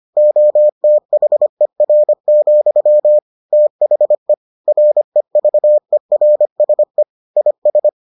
Morse code, heard as a keyed tone, about 25 wpm, 600 Hz. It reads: OTHER, THE REVERSE IS